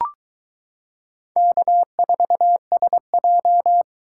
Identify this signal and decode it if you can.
Morse code, heard as a keyed tone, about 22 wpm, 700 Hz. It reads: K4SJ